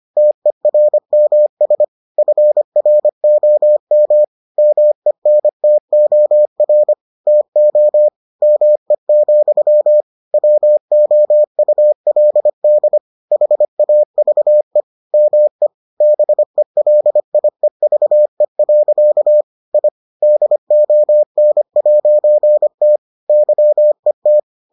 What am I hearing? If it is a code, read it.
Morse code, heard as a keyed tone, about 25 wpm, 600 Hz. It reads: TERMS FROM MENTOR TO ME, WOULD HAVE ME BELIEVE. I DON'T YET